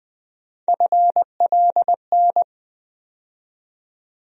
Morse code, heard as a keyed tone, about 20 wpm, 700 Hz. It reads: FLN
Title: USA Call Sign Postfixes - 3 Letters 20wpm